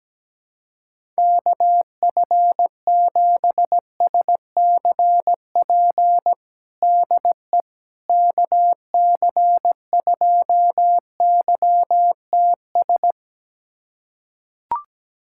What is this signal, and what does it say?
Morse code, heard as a keyed tone, about 17 wpm, 700 Hz. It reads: KF7SCP DE KC2YTS
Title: USA Callsign DE Callsign - 6 Characters - 17wpm